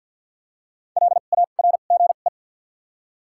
Morse code, heard as a keyed tone, about 53 wpm, spaced as for 37 wpm, 700 Hz. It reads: LARGE
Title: 2x Character Spacing - Top 200 Words - 50wpm